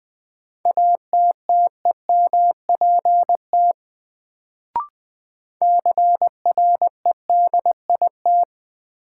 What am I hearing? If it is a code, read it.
Morse code, heard as a keyed tone, about 20 wpm, 700 Hz. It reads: ATTEMPT CREDIT
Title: Sentences from Top 1000 Words - New Word Drill - 20wpm